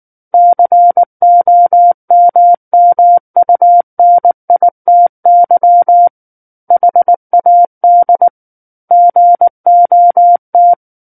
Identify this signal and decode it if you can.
Morse code, heard as a keyed tone, about 19 wpm, 700 Hz. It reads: COMMUNITY HAD GOT
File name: Book One / 10-Chapter 10 - 19 WPM